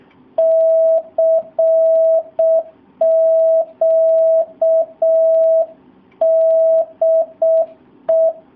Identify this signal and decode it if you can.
Morse code, heard as a keyed tone, about 6 wpm, 650 Hz. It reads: CQ DE
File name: jh8cbh / 2019cw / SP07- w